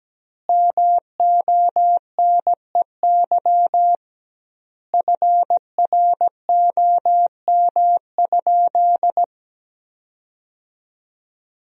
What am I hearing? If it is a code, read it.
Morse code, heard as a keyed tone, about 17 wpm, 700 Hz. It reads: MONEY FROM?
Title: 2x Word-Spacing - Sentences from Top 400 Words - 17wpm